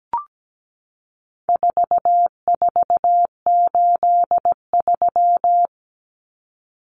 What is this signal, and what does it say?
Morse code, heard as a keyed tone, about 17 wpm, 700 Hz. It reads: 4483
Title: Common Cut Numbers - 4-Digit 17wpm